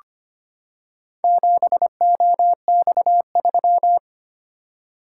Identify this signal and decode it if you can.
Morse code, heard as a keyed tone, about 25 wpm, 700 Hz. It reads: 7OX3
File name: practice / 25wpm / Four Letter-Number Combinations - 25wpm